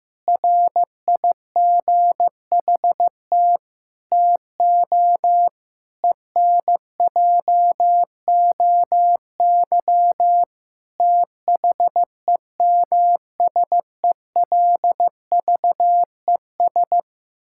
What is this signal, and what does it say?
Morse code, heard as a keyed tone, about 15 wpm, 700 Hz. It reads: RIGHT TO ENJOY THEMSELVES
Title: Sentences from Top 1000 Words - Review Part 5 - 15wpm